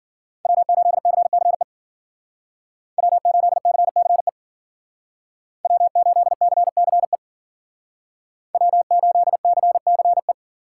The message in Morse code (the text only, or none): W8CCE W8CCE W8CCE W8CCE